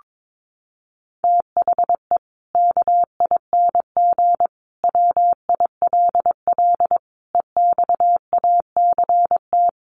Morse code, heard as a keyed tone, about 22 wpm, 700 Hz. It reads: THE KING WILL EXACT
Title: Sentences from Top 1300 Words 22wpm